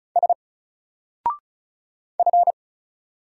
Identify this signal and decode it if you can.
Morse code, heard as a keyed tone, about 36 wpm, 700 Hz. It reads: S F
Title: Single Character (Letter:Number:Punctuation:Prosign) 35wpm